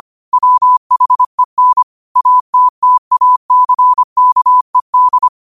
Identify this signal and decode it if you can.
Morse code, heard as a keyed tone, about 25 wpm, 1000 Hz. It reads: WHEN ATTACKED